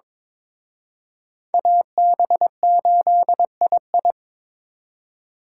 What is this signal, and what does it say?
Morse code, heard as a keyed tone, about 26 wpm, 700 Hz. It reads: AB8II
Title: Speed Racing - USA 2x2 Call Signs 22wpm